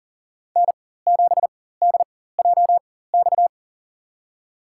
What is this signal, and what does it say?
Morse code, heard as a keyed tone, about 40 wpm, spaced as for 19 wpm, 700 Hz. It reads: N7DJX
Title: USA Call Signs - 5 Characters - 4x Character Spacing - 40wpm